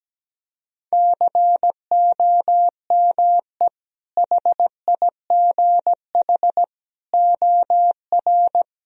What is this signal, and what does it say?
Morse code, heard as a keyed tone, about 17 wpm, 700 Hz. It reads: COME HIGH OR
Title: Sentences from Top 1000 Words - Review Part 1 - 17wpm